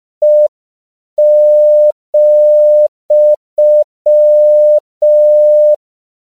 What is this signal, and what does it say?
Morse code, heard as a keyed tone, about 5 wpm, 600 Hz. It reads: E,
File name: CW / 5PPM